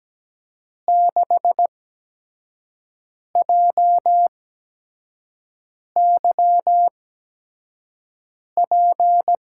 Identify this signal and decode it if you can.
Morse code, heard as a keyed tone, about 17 wpm, spaced as for 5 wpm, 700 Hz. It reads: 6JYP
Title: USA Call Signs - 6 Characters - 8x Character Spacing - 17wpm